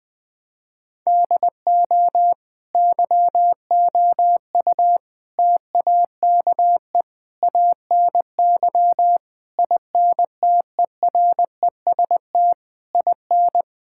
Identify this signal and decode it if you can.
Morse code, heard as a keyed tone, about 20 wpm, 700 Hz. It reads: DO YOU TAKE ANY INTEREST IN
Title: Sentences from Top 1000 Words - Review Part 6 - 20wpm